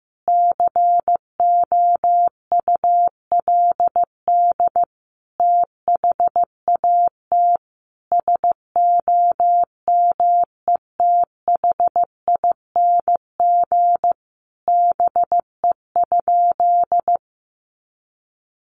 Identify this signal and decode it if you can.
Morse code, heard as a keyed tone, about 15 wpm, 700 Hz. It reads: COULD THAT SOMETHING BE?
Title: Sentences from Top 600 Words 15wpm